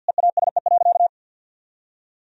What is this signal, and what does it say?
Morse code, heard as a keyed tone, about 52 wpm, 700 Hz. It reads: EASE.
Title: Sentences from Top 1000 Words - Review Part 1 - 50wpm